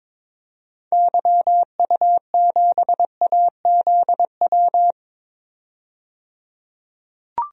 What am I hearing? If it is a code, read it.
Morse code, heard as a keyed tone, about 22 wpm, 700 Hz. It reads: YU7AZW
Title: DXCC-100 Call Signs - 6 Characters - 22wpm